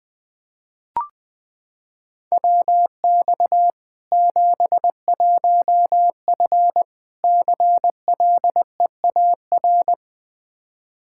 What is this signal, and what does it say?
Morse code, heard as a keyed tone, about 20 wpm, 700 Hz. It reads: WX 71F CLEAR
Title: K1USN SST QSO Practice 20wpm